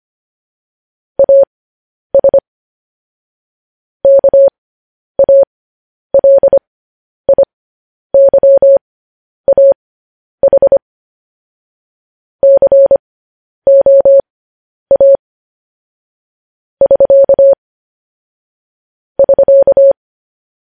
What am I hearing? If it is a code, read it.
Morse code, heard as a keyed tone, about 25 wpm, spaced as for 10 wpm, 550 Hz. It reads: AS KALIYAH COA <SK> <SK>
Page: news headlines in morse code at 10 wpm